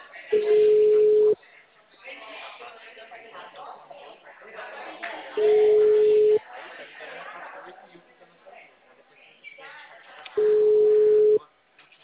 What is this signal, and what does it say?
Ringback tone (Europe, 425 Hz), with speech.